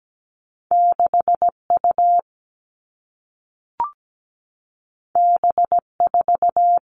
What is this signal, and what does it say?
Morse code, heard as a keyed tone, about 17 wpm, 700 Hz. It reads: 6U B4
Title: All Cut Numbers - 2-Digit 17wpm